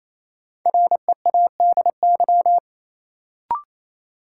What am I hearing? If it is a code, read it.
Morse code, heard as a keyed tone, about 28 wpm, 700 Hz. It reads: READY
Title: Sentences from Top 400 Words - New Word Drill - 28wpm